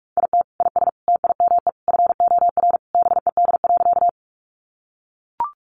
Morse code, heard as a keyed tone, about 44 wpm, 700 Hz. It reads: IT IS TIME FOR BED.
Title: Sentences from Top 500 Words 45wpm